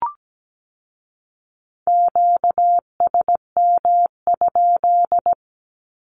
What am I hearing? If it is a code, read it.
Morse code, heard as a keyed tone, about 17 wpm, 700 Hz. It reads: QSM?
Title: Q-Codes with Spoken Meaning - 17wpm